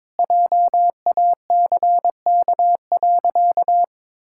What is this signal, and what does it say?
Morse code, heard as a keyed tone, about 22 wpm, 700 Hz. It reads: JACK.
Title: Pangram Challenge v2 - 22wpm